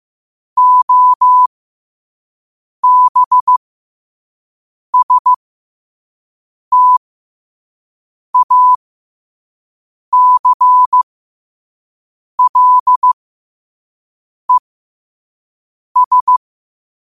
Morse code, heard as a keyed tone, about 15 wpm, spaced as for 5 wpm, 1000 Hz. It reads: OBSTACLES
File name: qotdfiles / 5wpm